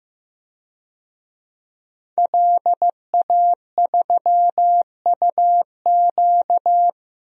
Morse code, heard as a keyed tone, about 16 wpm, 700 Hz. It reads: LA3UQ
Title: DXCC-100 Call Signs - 5 Characters - Repeated 2x - 15wpm